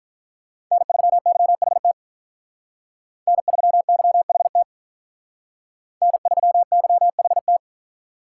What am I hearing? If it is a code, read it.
Morse code, heard as a keyed tone, about 45 wpm, 700 Hz. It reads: N3YHT N3YHT N3YHT